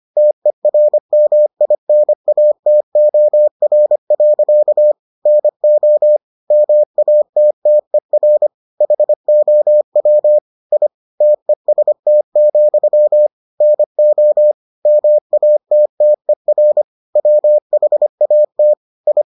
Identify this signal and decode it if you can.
Morse code, heard as a keyed tone, about 25 wpm, 600 Hz. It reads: TERMINATOR. NO MATTER HOW I TEST, NO MATTER WHAT I